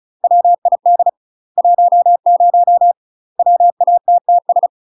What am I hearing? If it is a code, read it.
Morse code, heard as a keyed tone, about 34 wpm, 700 Hz. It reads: WID 10 WATTS
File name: UK MASTER QSOs TRAINING / 35WPM / UK MASTER QSOs 0040